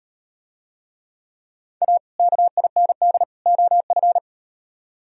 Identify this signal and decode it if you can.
Morse code, heard as a keyed tone, about 38 wpm, 700 Hz. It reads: A KIND OF